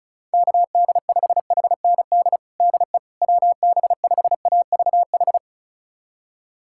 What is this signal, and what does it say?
Morse code, heard as a keyed tone, about 36 wpm, 700 Hz. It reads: KD5HND DE WB5AVH
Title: USA Callsign DE Callsign - 6 Characters - 35wpm